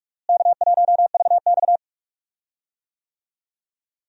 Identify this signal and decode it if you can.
Morse code, heard as a keyed tone, about 46 wpm, 700 Hz. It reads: K1VX
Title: USA Call Signs - 4 Characters - Repeated 3x - 45wpm